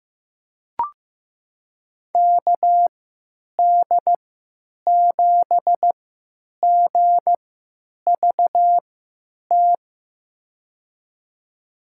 Morse code, heard as a keyed tone, about 15 wpm, spaced as for 9 wpm, 700 Hz. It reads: KD7GVT
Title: USA Call Signs - 6 Characters - 3x Character Spacing - 15wpm